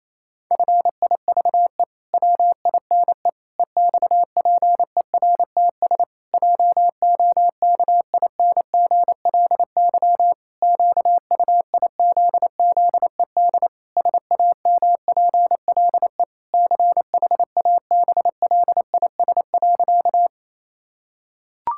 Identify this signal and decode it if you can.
Morse code, heard as a keyed tone, about 28 wpm, 700 Hz. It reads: FIVE WINE EXPERTS JOKINGLY QUIZZED SAMPLE CHABLIS.
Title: Pangram Challenge v2 - 28wpm